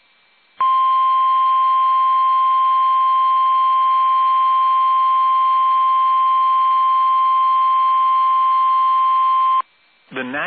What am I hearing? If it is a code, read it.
Emergency Alert System (SAME) tones (the 1050 Hz attention tone), with speech. It is heard from 0.6 to 9.6 s.